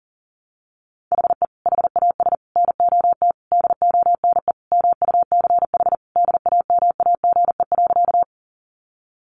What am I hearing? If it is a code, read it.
Morse code, heard as a keyed tone, about 40 wpm, 700 Hz. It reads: HE HAS NOT DONE MUCH DAMAGE.
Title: Sentences from Top 1100 Words 40wpm